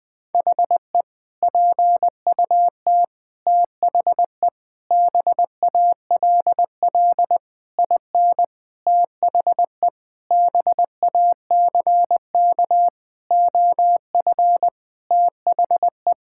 Morse code, heard as a keyed tone, about 20 wpm, 700 Hz. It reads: HE PUT THE BALL IN THE BACK OF THE
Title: Sentences with 8-Word Phrases - Spoken Beginning - 20wpm